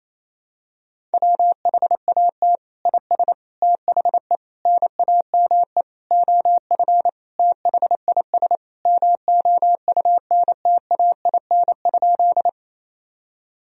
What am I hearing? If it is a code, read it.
Morse code, heard as a keyed tone, about 28 wpm, 700 Hz. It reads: WHAT IS THE NAME OF THIS MOUNTAIN?
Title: Sentences from Top 1000 Words - Review Part 1 - 28wpm